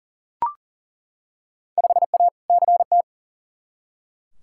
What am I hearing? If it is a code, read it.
Morse code, heard as a keyed tone, about 38 wpm, 700 Hz. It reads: 5A CT
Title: Field Day Exchanges - 40wpm